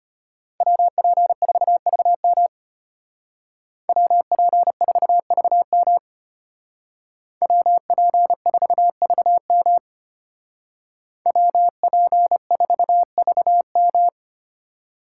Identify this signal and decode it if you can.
Morse code, heard as a keyed tone, about 33 wpm, 700 Hz. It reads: WP4VM WP4VM WP4VM WP4VM